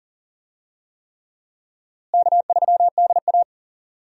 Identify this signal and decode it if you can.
Morse code, heard as a keyed tone, about 40 wpm, 700 Hz. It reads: K3DA
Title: USA Call Signs - 4 Characters - Repeated 3x - 40wpm